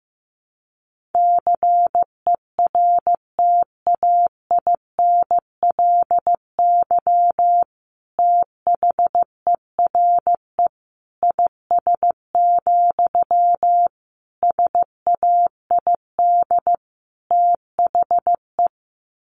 Morse code, heard as a keyed tone, about 15 wpm, 700 Hz. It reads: CERTAINLY THERE IS, SAID THE
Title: Sentences from Top 1100 Words 15wpm